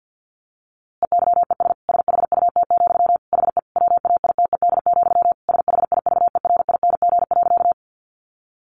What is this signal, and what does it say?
Morse code, heard as a keyed tone, about 50 wpm, 700 Hz. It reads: EYES SHUT, HE WAITED, SHIVERING.
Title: A Sound Of Thunder 50wpm